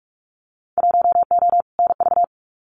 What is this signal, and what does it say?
Morse code, heard as a keyed tone, about 46 wpm, 700 Hz. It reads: 1O NV
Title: Winter Field Day Exchanges Encore 45wpm